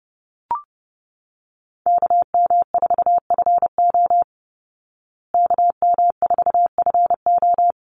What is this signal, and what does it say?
Morse code, heard as a keyed tone, about 28 wpm, 700 Hz. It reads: KM4FO KM4FO
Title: Call Signs Repeated 2x - 30wpm